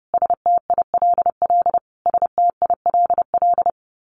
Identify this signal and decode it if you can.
Morse code, heard as a keyed tone, about 30 wpm, 700 Hz. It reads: STILL STILL
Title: Top 300 Words Repeated 30wpm